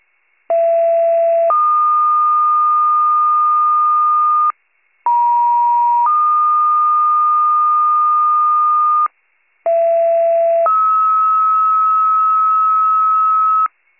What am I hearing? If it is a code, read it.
Two-tone page, heard as a one-second tone, then a three-second tone, about 670 then 1220 Hz, about 950 then 1220 Hz, about 670 then 1290 Hz. It is heard from 0.5 to 13.7 s.